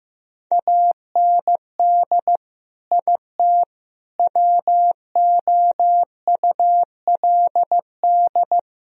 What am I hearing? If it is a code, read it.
Morse code, heard as a keyed tone, about 15 wpm, 700 Hz. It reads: AND IT WOULD
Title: Sentences from Top 1000 Words - Review Part 4 - 15wpm